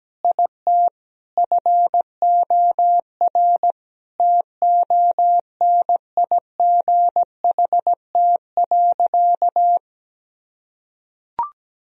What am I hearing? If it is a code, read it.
Morse code, heard as a keyed tone, about 17 wpm, 700 Hz. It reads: IT FOR TONIGHT.
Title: Sentences from Top 1000 Words - Review Part 3 - 17wpm